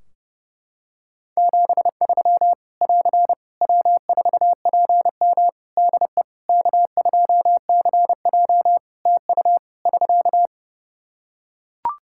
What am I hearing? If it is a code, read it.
Morse code, heard as a keyed tone, about 30 wpm, 700 Hz. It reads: 73 <AR> W4PM DE K2CJ TU <SK>
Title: K1USN SST QSO Practice 30wpm